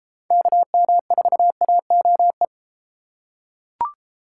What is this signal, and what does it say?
Morse code, heard as a keyed tone, about 31 wpm, 700 Hz. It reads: KM4AOE